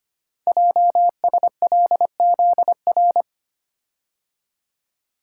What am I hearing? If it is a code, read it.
Morse code, heard as a keyed tone, about 25 wpm, 700 Hz. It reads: JSLZR
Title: Code Groups - 5 Letters - 25wpm